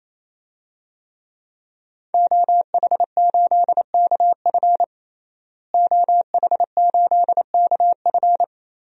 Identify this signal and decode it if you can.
Morse code, heard as a keyed tone, about 28 wpm, 700 Hz. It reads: OH8KF OH8KF
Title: DXCC-100 Call Signs - 5 Characters - Repeated 3x - 28wpm